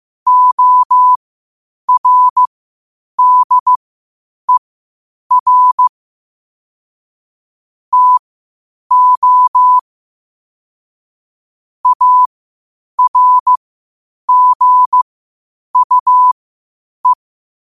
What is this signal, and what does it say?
Morse code, heard as a keyed tone, about 15 wpm, spaced as for 8 wpm, 1000 Hz. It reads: ORDER TO ARGUE
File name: qotdfiles / 8wpm